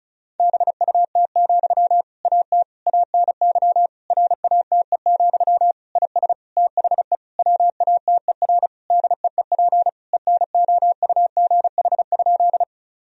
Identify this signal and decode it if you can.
Morse code, heard as a keyed tone, about 36 wpm, 700 Hz. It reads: BUT, AT ANY RATE, IS THE WATER DEEP ENOUGH?
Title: Sentences from Top 1000 Words - Review Part 4 - 35wpm